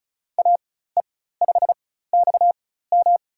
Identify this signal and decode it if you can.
Morse code, heard as a keyed tone, about 36 wpm, spaced as for 16 wpm, 700 Hz. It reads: AE5XM
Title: USA Call Signs - 5 Characters - 4x Character Spacing - 35wpm